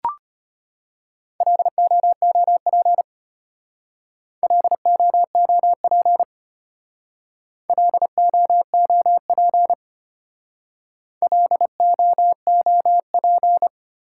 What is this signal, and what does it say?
Morse code, heard as a keyed tone, about 33 wpm, 700 Hz. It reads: LOOP LOOP LOOP LOOP